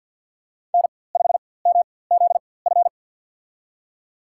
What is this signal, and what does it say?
Morse code, heard as a keyed tone, about 47 wpm, spaced as for 23 wpm, 700 Hz. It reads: N5MZF
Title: USA Call Signs - 5 Characters - 4x Character Spacing - 50wpm